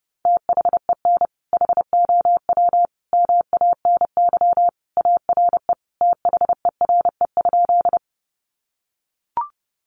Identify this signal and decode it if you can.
Morse code, heard as a keyed tone, about 30 wpm, 700 Hz. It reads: THEN HOW MANY ARE THERE?
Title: Sentences from Top 100 Words 30wpm